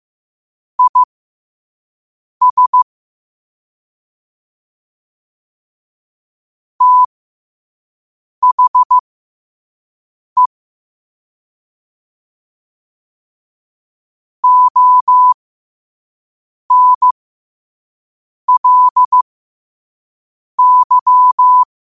Morse code, heard as a keyed tone, about 15 wpm, spaced as for 5 wpm, 1000 Hz. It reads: IS THE ONLY
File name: qotdfiles / 5wpm